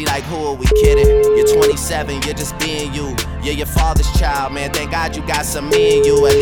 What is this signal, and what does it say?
Ringback tone (Europe, 425 Hz), with noise.